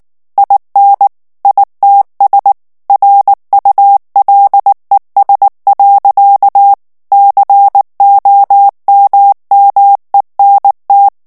Morse code, heard as a keyed tone, about 19 wpm, 800 Hz. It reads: IN ITS RULES. COMMENT
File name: ANGLAIS texte clair 3 vitesse 18 mots minute